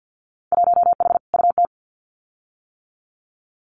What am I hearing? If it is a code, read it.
Morse code, heard as a keyed tone, about 50 wpm, 700 Hz. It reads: 1H UT
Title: Winter Field Day Exchanges Encore 50wpm